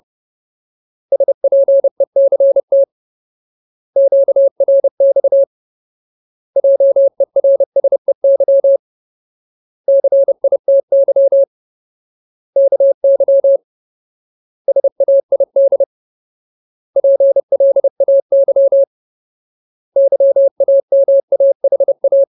Morse code, heard as a keyed tone, about 30 wpm, 550 Hz. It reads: SPECT QRX JERSEY CITY KY SAID PLAY YAMAHA
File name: Ultimate Word List / 30WPM Ultimate